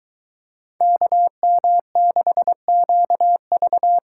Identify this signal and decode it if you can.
Morse code, heard as a keyed tone, about 23 wpm, 700 Hz. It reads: KM6QV